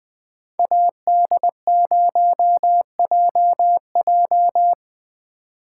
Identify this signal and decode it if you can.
Morse code, heard as a keyed tone, about 19 wpm, 700 Hz. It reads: AD0JJ